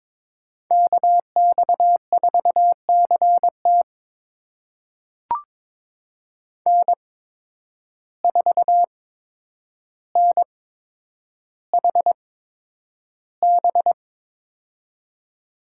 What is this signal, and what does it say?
Morse code, heard as a keyed tone, about 22 wpm, spaced as for 9 wpm, 700 Hz. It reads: KX4CT N4NHB
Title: USA Call Signs - 5 Characters - 8x Character Spacing - 22wpm